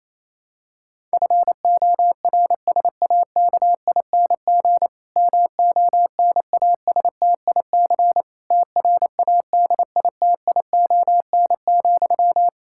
Morse code, heard as a keyed tone, about 27 wpm, 700 Hz. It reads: FORSAKING MONASTIC TRADITION,
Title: Pangram Challenge v2 - 28wpm